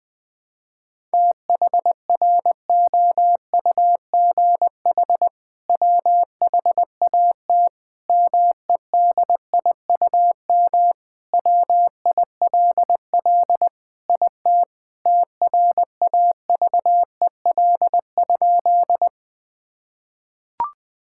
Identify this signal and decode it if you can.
Morse code, heard as a keyed tone, about 20 wpm, 700 Hz. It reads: THROUGH WHAT MEDIUM WILL IT TRAVEL?
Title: Sentences from Top 1000 Words - Review Part 6 - 20wpm